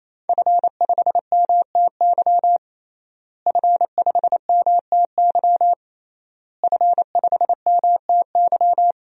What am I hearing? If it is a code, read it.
Morse code, heard as a keyed tone, about 28 wpm, 700 Hz. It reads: F5MTY F5MTY F5MTY
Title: DXCC-100 Call Signs - 5 Characters - Repeated 3x - 28wpm